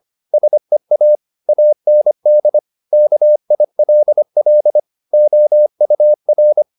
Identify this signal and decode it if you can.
Morse code, heard as a keyed tone, about 25 wpm, 600 Hz. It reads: SEA AND KILL OUR